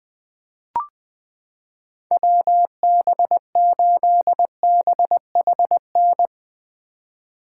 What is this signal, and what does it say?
Morse code, heard as a keyed tone, about 20 wpm, 700 Hz. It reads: WB8BHN
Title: 2023 SOTA Chaser Call Signs - 20wpm